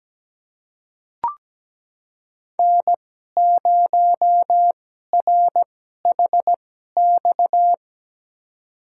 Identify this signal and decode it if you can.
Morse code, heard as a keyed tone, about 17 wpm, spaced as for 12 wpm, 700 Hz. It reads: N0RHX
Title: USA Call Signs - 5 Characters - 2x Character Spacing - 17wpm